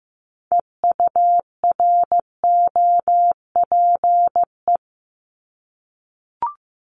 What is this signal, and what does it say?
Morse code, heard as a keyed tone, about 15 wpm, 700 Hz. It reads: EUROPE
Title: Multi-letter Phonemes 15wpm